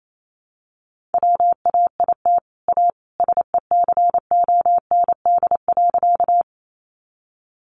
Morse code, heard as a keyed tone, about 28 wpm, 700 Hz. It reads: WAIT A SECOND.
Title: Sentences from Top 500 Words 28wpm